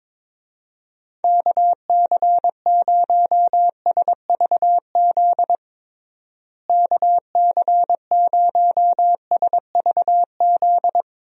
Morse code, heard as a keyed tone, about 22 wpm, 700 Hz. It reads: KC0SVZ KC0SVZ
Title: USA Call Signs - 6 Characters - Repeated 3x - 22wpm